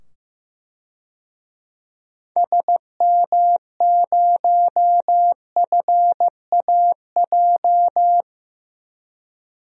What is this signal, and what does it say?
Morse code, heard as a keyed tone, about 15 wpm, 700 Hz. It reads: SM0FAJ
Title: DXCC-100 Call Signs - 6 Characters - 15wpm